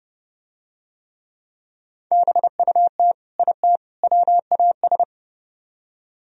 Morse code, heard as a keyed tone, about 30 wpm, 700 Hz. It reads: BUT IT WAS